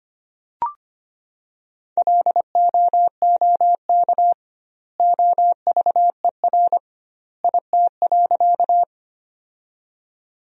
Morse code, heard as a keyed tone, about 25 wpm, 700 Hz. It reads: LOOK OVER IT.
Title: 2x Word-Spacing - Sentences from Top 100 Words - 25wpm